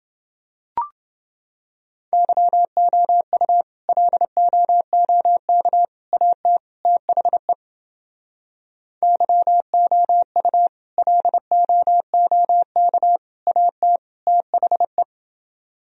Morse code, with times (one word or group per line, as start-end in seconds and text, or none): 2.1-3.6 YOU
3.9-5.8 LOOK
6.1-6.6 AT
6.9-7.5 THE
9.0-10.7 YOU
11.0-13.2 LOOK
13.5-14.0 AT
14.3-15.0 THE